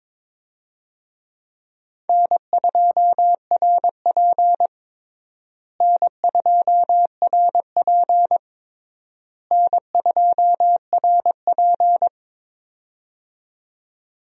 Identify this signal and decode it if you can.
Morse code, heard as a keyed tone, about 22 wpm, 700 Hz. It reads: N2RP N2RP N2RP
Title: USA Call Signs - 4 Characters - Repeated 3x - 22wpm